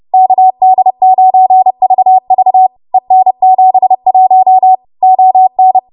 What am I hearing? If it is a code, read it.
Morse code, heard as a keyed tone, about 30 wpm, 750 Hz. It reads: KD9VV EN71 ON
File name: Morse / 30WPM